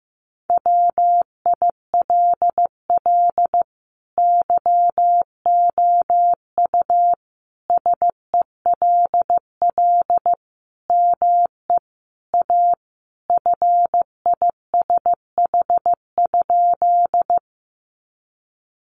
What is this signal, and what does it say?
Morse code, heard as a keyed tone, about 15 wpm, 700 Hz. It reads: WILL YOU SELL ME A FISH?
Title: Sentences from Top 1100 Words 15wpm